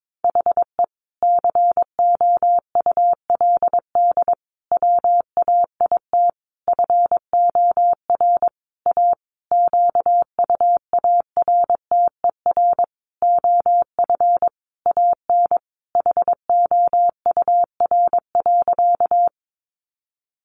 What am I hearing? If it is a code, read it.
Morse code, heard as a keyed tone, about 22 wpm, 700 Hz. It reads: HE COULD WAIT FOR A QUARTER OF AN HOUR.
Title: Sentences from Top 1300 Words 22wpm